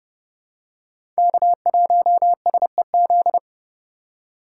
Morse code, heard as a keyed tone, about 30 wpm, 700 Hz. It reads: K1SEZ
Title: USA Call Signs - 5 Characters - 2x Character Spacing - 30wpm